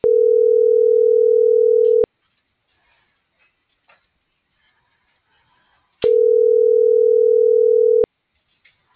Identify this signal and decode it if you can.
Ringback tone (North America), 440 + 480 Hz.